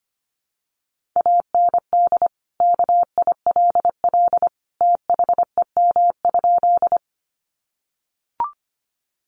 Morse code, heard as a keyed tone, about 25 wpm, 700 Hz. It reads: AND KILL THEM?
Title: Sentences from Top 1400 Words 25wpm